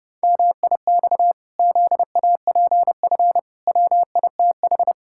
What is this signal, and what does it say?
Morse code, heard as a keyed tone, about 30 wpm, 700 Hz. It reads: MIX ZAPF WITH
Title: Pangram Challenge v2 - 30wpm